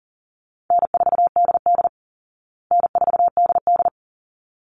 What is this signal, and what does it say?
Morse code, heard as a keyed tone, about 37 wpm, 700 Hz. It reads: N4DD N4DD
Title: Easy Call Signs Repeated 2x - 40wpm